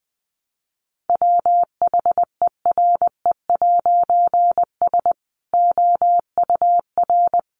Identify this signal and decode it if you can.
Morse code, heard as a keyed tone, about 20 wpm, 700 Hz. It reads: WHERE'S OUR
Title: A Sound Of Thunder 20wpm